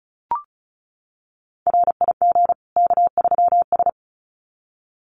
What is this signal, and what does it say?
Morse code, heard as a keyed tone, about 34 wpm, 700 Hz. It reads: RIG K3S
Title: Common QSO Phrases 35wpm